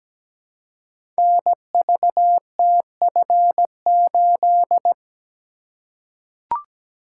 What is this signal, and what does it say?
Morse code, heard as a keyed tone, about 17 wpm, 700 Hz. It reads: NVTF8
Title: Code Groups - 5 Letter:Numbers - 17wpm